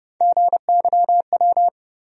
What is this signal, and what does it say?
Morse code, heard as a keyed tone, about 30 wpm, 700 Hz. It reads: GYW
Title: USA Call Sign Postfixes - 3 Letters 30wpm